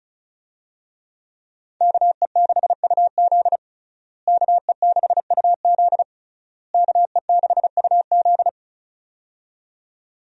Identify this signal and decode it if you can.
Morse code, heard as a keyed tone, about 36 wpm, 700 Hz. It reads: KE6UZ KE6UZ KE6UZ
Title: USA Call Signs - 5 Characters - Repeated 3x - 35wpm